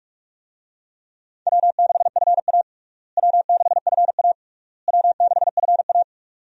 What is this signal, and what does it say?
Morse code, heard as a keyed tone, about 41 wpm, 700 Hz. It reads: W6FA W6FA W6FA
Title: USA Call Signs - 4 Characters - Repeated 3x - 45wpm